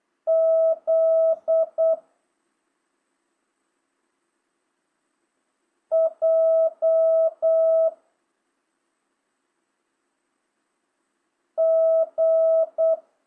Morse code, heard as a keyed tone, about 8 wpm, 650 Hz. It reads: Z J G